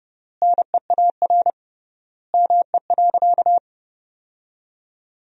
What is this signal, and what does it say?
Morse code, heard as a keyed tone, about 30 wpm, 700 Hz. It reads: NEAR ME.
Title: 3x Word-Spacing - Sentences from Top 300 Words - 30wpm